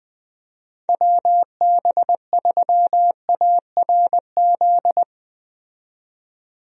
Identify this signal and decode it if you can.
Morse code, heard as a keyed tone, about 20 wpm, 700 Hz. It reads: WB3ARZ
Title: USA Call Signs - 6 Characters - 20wpm